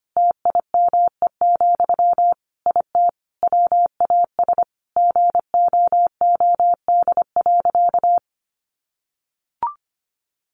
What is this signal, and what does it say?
Morse code, heard as a keyed tone, about 25 wpm, 700 Hz. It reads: TIME, IT WAS GOOD.
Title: Sentences from Top 1300 Words 25wpm